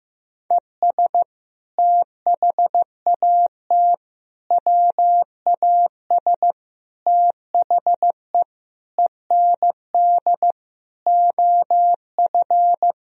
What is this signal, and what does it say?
Morse code, heard as a keyed tone, about 16 wpm, 700 Hz. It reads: ES THAT WAS THE END OF
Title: Sentences from Top 200 Words and Abbreviations 15wpm